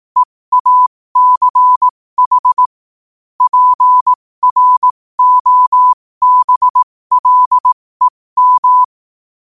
Morse code, heard as a keyed tone, about 18 wpm, spaced as for 15 wpm, 1000 Hz. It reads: EACH PROBLEM
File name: 15wpm